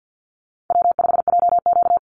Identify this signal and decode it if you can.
Morse code, heard as a keyed tone, about 50 wpm, 700 Hz. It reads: W5JQ
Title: Easy Call Signs - 50wpm